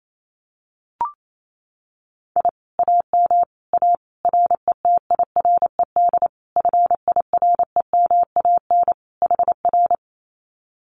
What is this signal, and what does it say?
Morse code, heard as a keyed tone, about 28 wpm, 700 Hz. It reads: I AM A RETIRED FIREMAN HR
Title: Common QSO Phrases Encore 28wpm